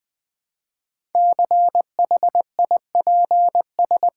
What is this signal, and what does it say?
Morse code, heard as a keyed tone, about 20 wpm, 700 Hz. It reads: CHIPS
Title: Easy Words with Hints - 20wpm